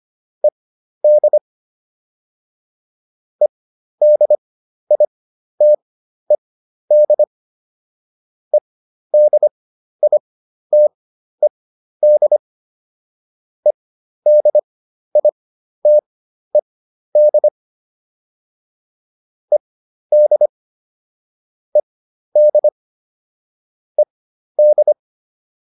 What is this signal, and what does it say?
Morse code, heard as a keyed tone, about 25 wpm, spaced as for 12 wpm, 600 Hz. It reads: ED EDITED EDITED EDITED ED ED ED
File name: ED6 12